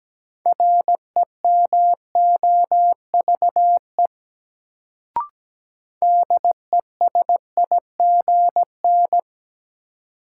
Morse code, heard as a keyed tone, about 17 wpm, 700 Hz. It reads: REMOVE DESIGN
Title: Sentences from Top 800 Words - New Word Drill - 17wpm